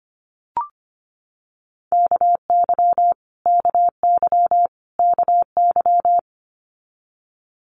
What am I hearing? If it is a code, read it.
Morse code, heard as a keyed tone, about 25 wpm, 700 Hz. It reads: KY KY KY
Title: Multi-letter Phonemes Repeated 3x 25wpm